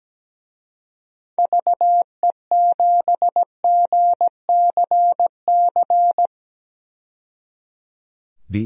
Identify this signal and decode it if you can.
Morse code, heard as a keyed tone, about 17 wpm, 700 Hz. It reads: VE7GCC
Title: DXCC-100 Call Signs - 6 Characters - 17wpm